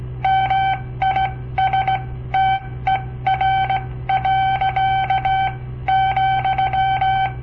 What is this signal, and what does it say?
Morse code, heard as a keyed tone, about 17 wpm, spaced as for 14 wpm, 770 Hz. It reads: MISTER. ,
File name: papertapes / local / Code Practice-01